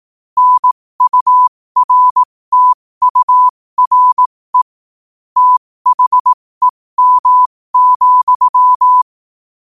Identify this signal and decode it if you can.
Morse code, heard as a keyed tone, about 18 wpm, spaced as for 15 wpm, 1000 Hz. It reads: NURTURE THEM,